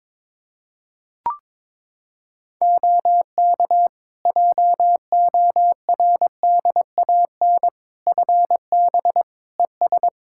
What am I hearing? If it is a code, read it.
Morse code, heard as a keyed tone, about 22 wpm, 700 Hz. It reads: OK JORDAN FB ES
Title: Practice QSO - Exchange 2 - Info - 22wpm